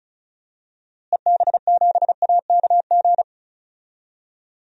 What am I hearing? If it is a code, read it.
Morse code, heard as a keyed tone, about 36 wpm, 700 Hz. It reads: EB7AKG
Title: DXCC-100 Call Signs - 6 Characters - Repeated 3x - 35wpm